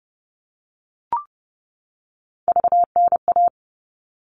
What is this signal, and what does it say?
Morse code, heard as a keyed tone, about 30 wpm, 700 Hz. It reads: VNA
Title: All Cut Numbers - 3-Digit 30wpm